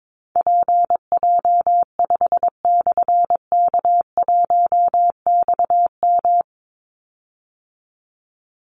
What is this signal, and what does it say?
Morse code, heard as a keyed tone, about 22 wpm, 700 Hz. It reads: PJ5/K1XM
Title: Call Signs Repeated 3x - 22wpm